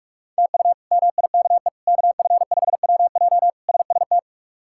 Morse code, heard as a keyed tone, about 41 wpm, 700 Hz. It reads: TU MIKE KF5WJ SST
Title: K1USN SST QSO Practice - 45wpm